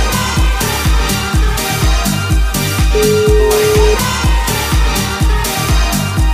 Ringback tone (Europe, 425 Hz), with noise.